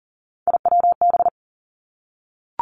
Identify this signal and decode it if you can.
Morse code, heard as a keyed tone, about 42 wpm, 700 Hz. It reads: IWB
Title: Three Challenging Letter-Number Combinations 40wpm